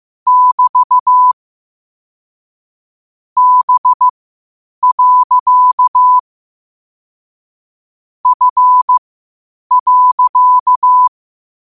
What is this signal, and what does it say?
Morse code, heard as a keyed tone, about 15 wpm, spaced as for 8 wpm, 1000 Hz. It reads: <BT> B. F.
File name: qotdfiles / 8wpm